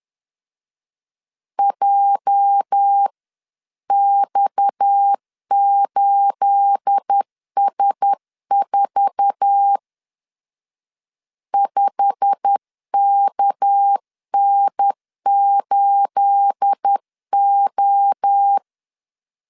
Morse code, heard as a keyed tone, about 10 wpm, 800 Hz. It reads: JX8S4 5KN8O
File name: LEZIONI / 9 28 10wpm 2842